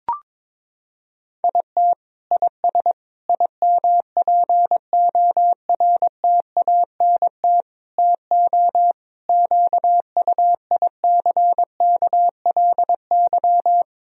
Morse code, heard as a keyed tone, about 22 wpm, 700 Hz. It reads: IT IS IMPORTANT TO QUICKLY
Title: Sentences from Top 1000 Words - Review Part 4 - 22wpm